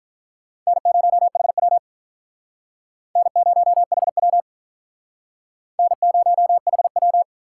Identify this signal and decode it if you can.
Morse code, heard as a keyed tone, about 44 wpm, 700 Hz. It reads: N0HW N0HW N0HW